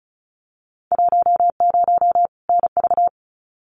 Morse code, heard as a keyed tone, about 36 wpm, 700 Hz. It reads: 10 NV
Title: Winter Field Day Exchanges 35wpm